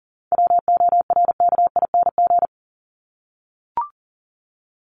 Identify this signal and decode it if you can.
Morse code, heard as a keyed tone, about 40 wpm, 700 Hz. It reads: WORKING
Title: Top 501-750 Words 40wpm